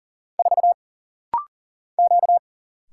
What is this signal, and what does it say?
Morse code, heard as a keyed tone, about 40 wpm, 700 Hz. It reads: 4 Q
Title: Single Letter-Number Alt - Rapid-Fire 40wpm